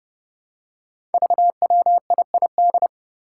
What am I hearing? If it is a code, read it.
Morse code, heard as a keyed tone, about 30 wpm, 700 Hz. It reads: VWIID
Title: Five Challenging Letter-Number Combinations 30wpm